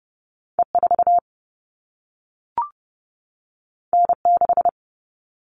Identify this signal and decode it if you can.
Morse code, heard as a keyed tone, about 30 wpm, 700 Hz. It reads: E4 N6
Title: All Cut Numbers - 2-Digit 30wpm